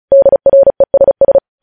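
Morse code, heard as a keyed tone, about 36 wpm, 550 Hz. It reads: DRESS
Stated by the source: Morse practice sending news headlines at 35 wpm.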